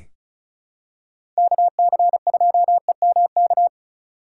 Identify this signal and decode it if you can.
Morse code, heard as a keyed tone, about 36 wpm, 700 Hz. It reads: KC2EMK
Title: USA Call Signs - 3x Character Spacing - 35wpm